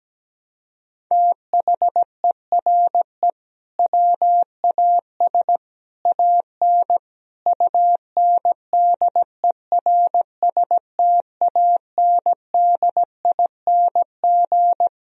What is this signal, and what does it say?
Morse code, heard as a keyed tone, about 17 wpm, 700 Hz. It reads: THERE WAS AN UNDERSTANDING
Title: Sentences from Top 1000 Words - Review Part 3 - 17wpm